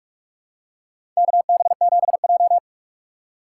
Morse code, heard as a keyed tone, about 44 wpm, 700 Hz. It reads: KB7J
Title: USA Call Signs - 4x Character Spacing - 45wpm